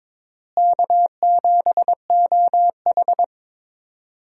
Morse code, heard as a keyed tone, about 22 wpm, 700 Hz. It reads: K7OH
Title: USA Call Signs - 4 Characters - Repeated 3x - 22wpm